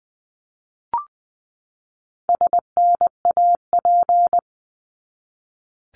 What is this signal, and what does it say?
Morse code, heard as a keyed tone, about 20 wpm, 700 Hz. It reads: SNAP
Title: 260 Lesson 42e - Words 20-20wpm